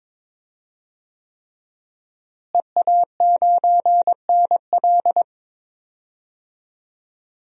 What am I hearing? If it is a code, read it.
Morse code, heard as a keyed tone, about 23 wpm, 700 Hz. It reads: EA9NL
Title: DXCC-100 Call Signs - 5 Characters - Repeated 3x - 22wpm